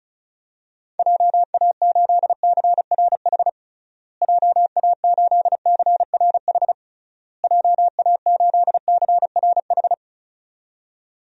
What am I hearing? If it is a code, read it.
Morse code, heard as a keyed tone, about 34 wpm, 700 Hz. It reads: JA8CRH JA8CRH JA8CRH
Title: DXCC-100 Call Signs - 6 Characters - Repeated 3x - 35wpm